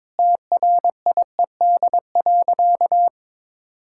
Morse code, heard as a keyed tone, about 22 wpm, 700 Hz. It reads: TRIED.
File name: Sentences with 7-Word Phrases - 22wpm